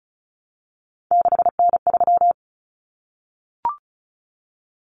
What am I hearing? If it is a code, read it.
Morse code, heard as a keyed tone, about 34 wpm, 700 Hz. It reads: 6N3
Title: Common Cut Numbers - 3-Digit 35wpm